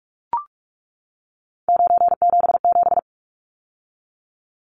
Morse code, heard as a keyed tone, about 44 wpm, 700 Hz. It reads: 977
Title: Numbers - 3-Digit 45wpm